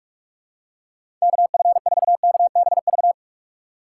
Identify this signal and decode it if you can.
Morse code, heard as a keyed tone, about 46 wpm, 700 Hz. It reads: KF4KBV
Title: USA Call Signs - 6x Character Spacing - 45wpm